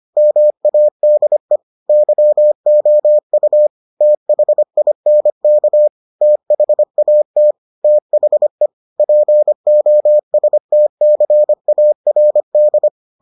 Morse code, heard as a keyed tone, about 25 wpm, 600 Hz. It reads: MADE YOU THINK THAT THE POSTCARD